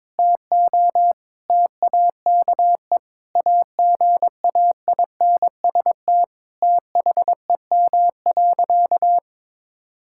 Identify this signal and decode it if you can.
Morse code, heard as a keyed tone, about 21 wpm, 700 Hz. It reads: TO TAKE AGAINST THEM.
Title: Sentences from Top 1000 Words - Review Part 6 - 22wpm